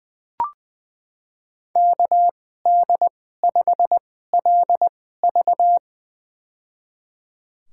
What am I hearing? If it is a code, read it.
Morse code, heard as a keyed tone, about 20 wpm, spaced as for 14 wpm, 700 Hz. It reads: KD5LV
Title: USA Call Signs - 5 Characters - 2x Character Spacing - 20wpm